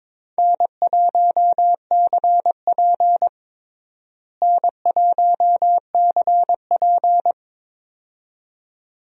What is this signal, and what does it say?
Morse code, heard as a keyed tone, about 22 wpm, 700 Hz. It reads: N1CP N1CP
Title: USA Call Signs - 4 Characters - Repeated 2x - 22wpm